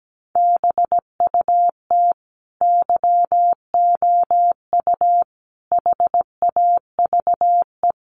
Morse code, heard as a keyed tone, about 17 wpm, 700 Hz. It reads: BUT YOU HAVE
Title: Sentences from Top 1100 Words 17wpm